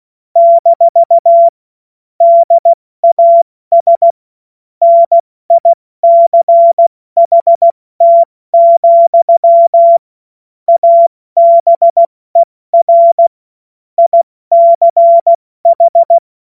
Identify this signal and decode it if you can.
Morse code, heard as a keyed tone, about 16 wpm, 680 Hz. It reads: - DAS NICHT, ABER ICH
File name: Morsekurs / Witze / Witz 26 März 2024 70 F80